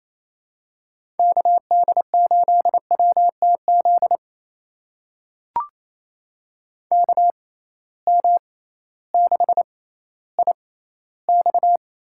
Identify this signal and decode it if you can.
Morse code, heard as a keyed tone, about 27 wpm, spaced as for 15 wpm, 700 Hz. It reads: KD8WTZ KM6IX
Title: USA Call Signs - 6 Characters - 6x Character Spacing - 28wpm